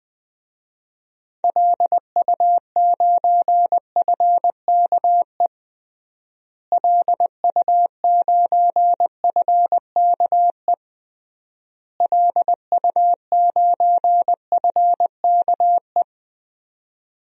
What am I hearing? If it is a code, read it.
Morse code, heard as a keyed tone, about 20 wpm, 700 Hz. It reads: LU9FKE LU9FKE LU9FKE
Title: DXCC-100 Call Signs - 6 Characters - Repeated 3x - 20wpm